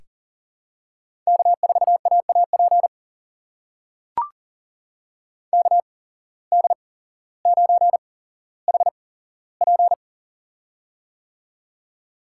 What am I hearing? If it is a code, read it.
Morse code, heard as a keyed tone, about 40 wpm, spaced as for 17 wpm, 700 Hz. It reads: K4AAP KD9HP
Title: USA Call Signs - 5 Characters - 8x Character Spacing - 40wpm